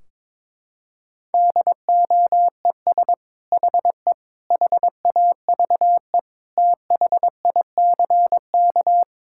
Morse code, heard as a keyed tone, about 22 wpm, 700 Hz. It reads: DOES HE HAVE THICK
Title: Sentences from Top 1600 Words 22wpm